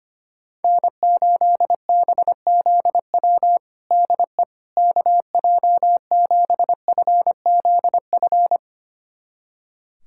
Morse code, heard as a keyed tone, about 25 wpm, 700 Hz. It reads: N8BZW DE KJ7FZF
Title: USA Callsign DE Callsign 25wpm